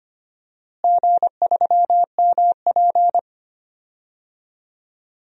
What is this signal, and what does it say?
Morse code, heard as a keyed tone, about 25 wpm, 700 Hz. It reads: G3MP
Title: DXCC-100 Call Signs - 4 Characters - Repeated 2x - 25wpm